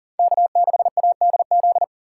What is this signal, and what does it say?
Morse code, heard as a keyed tone, about 40 wpm, 700 Hz. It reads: K6ADZ
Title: USA Call Signs - 5 Characters - 4x Character Spacing - 40wpm